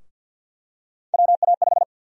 Morse code, heard as a keyed tone, about 50 wpm, 700 Hz. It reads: WA5
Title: USA Call Sign Prefixes - Repeated 2x - 50wpm